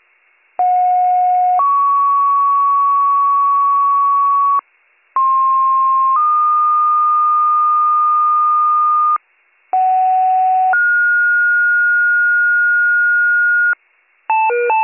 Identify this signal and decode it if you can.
Two-tone page, heard as a one-second tone, then a three-second tone, about 730 then 1120 Hz, about 1060 then 1220 Hz, about 750 then 1430 Hz. It is heard from 0.6 to 13.7 s.